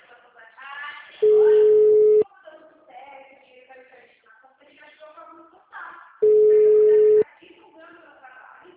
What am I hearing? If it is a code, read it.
Ringback tone (Europe, 425 Hz), with speech.
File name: out- -2002- - - .31623